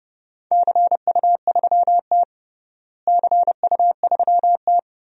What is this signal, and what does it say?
Morse code, heard as a keyed tone, about 32 wpm, 700 Hz. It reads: CU3T CU3T
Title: DXCC-100 Call Signs - 4 Characters - Repeated 2x - 30wpm